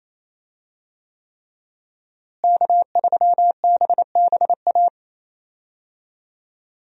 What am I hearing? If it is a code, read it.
Morse code, heard as a keyed tone, about 27 wpm, 700 Hz. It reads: K3BBA
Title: USA Call Signs - 5 Characters - Repeated 3x - 28wpm